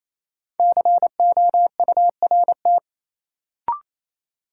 Morse code, heard as a keyed tone, about 28 wpm, 700 Hz. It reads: COURT
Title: Sentences from Top 1600 Words - New Word Drill Alt - 28wpm